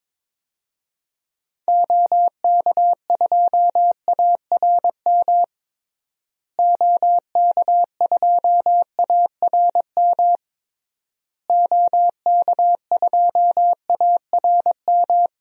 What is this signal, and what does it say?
Morse code, heard as a keyed tone, about 22 wpm, 700 Hz. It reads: OK2ARM OK2ARM OK2ARM
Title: DXCC-100 Call Signs - 6 Characters - Repeated 3x - 22wpm